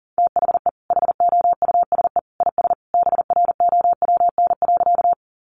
Morse code, heard as a keyed tone, about 40 wpm, 700 Hz. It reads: THE HOUSE IS BROWN.
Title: Sentences from Top 1200 Words 40wpm